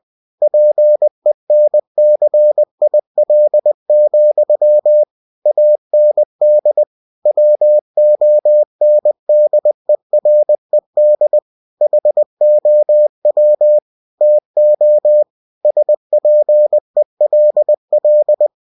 Morse code, heard as a keyed tone, about 20 wpm, 600 Hz. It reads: PENCIL, AND WONDERED HOW TO SPELL
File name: cw-books / winnie-the-pooh / winnie-20wpm-page-0051